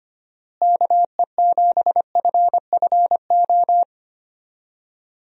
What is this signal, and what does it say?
Morse code, heard as a keyed tone, about 25 wpm, 700 Hz. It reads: KE7FFO
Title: USA Call Signs - 6 Characters - Repeated 3x - 25wpm